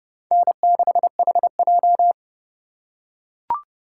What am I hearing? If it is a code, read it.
Morse code, heard as a keyed tone, about 29 wpm, 700 Hz. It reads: N6HJ